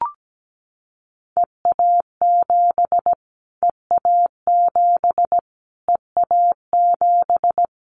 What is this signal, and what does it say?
Morse code, heard as a keyed tone, about 17 wpm, 700 Hz. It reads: EA7 EA7 EA7
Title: Callsign Prefixes for Top 25 DXCC - Repeated 3x - 17wpm